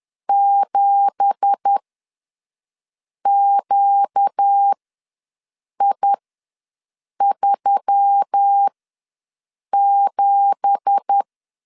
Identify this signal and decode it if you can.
Morse code, heard as a keyed tone, about 10 wpm, spaced as for 5 wpm, 800 Hz. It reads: 7QI37
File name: LEZIONI / 9 2867 ws 10wpm 4215